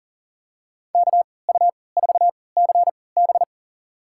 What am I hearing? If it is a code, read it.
Morse code, heard as a keyed tone, about 40 wpm, spaced as for 23 wpm, 700 Hz. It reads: KU4CB
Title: USA Call Signs - 3x Character Spacing - 40wpm